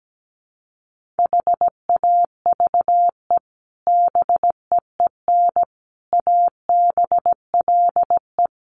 Morse code, heard as a keyed tone, about 17 wpm, 700 Hz. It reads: HAVE BEEN ABLE
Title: Sets of 4 Words 17wpm